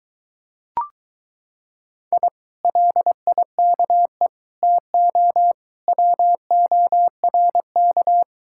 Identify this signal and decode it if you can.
Morse code, heard as a keyed tone, about 23 wpm, 700 Hz. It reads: I LIKE TO WORK